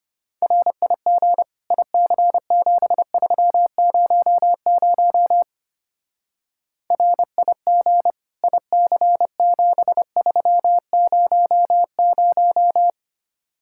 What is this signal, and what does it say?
Morse code, heard as a keyed tone, about 27 wpm, 700 Hz. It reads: RIG IC7300 RIG IC7300
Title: Speed-Racing Common QSO Phrases - 25wpm